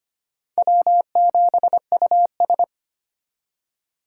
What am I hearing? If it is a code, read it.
Morse code, heard as a keyed tone, about 25 wpm, 700 Hz. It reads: W7US
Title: USA Call Signs - 3x Character Spacing - 25wpm